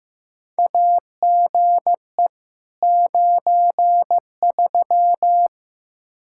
Morse code, heard as a keyed tone, about 15 wpm, 700 Hz. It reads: AGE 93
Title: K1USN SST QSO Practice 15wpm